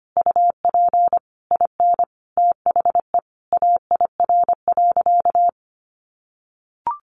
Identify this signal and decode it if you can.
Morse code, heard as a keyed tone, about 25 wpm, 700 Hz. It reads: UP IN THE AIR.
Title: Easy Phrases 25wpm